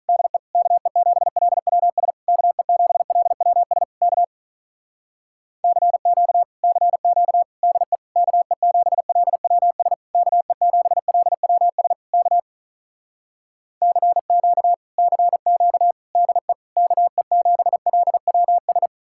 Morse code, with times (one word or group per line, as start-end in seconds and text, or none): 0.1-0.4 DE
0.5-2.1 KE7LWS
2.3-3.8 KE7LWS
4.0-4.2 K
5.6-6.4 CQ
6.6-7.4 CQ
7.6-8.0 DE
8.2-9.9 KE7LWS
10.2-11.9 KE7LWS
12.1-12.4 K
13.8-14.7 CQ
15.0-15.9 CQ
16.2-16.5 DE
16.8-18.9 KE7LWS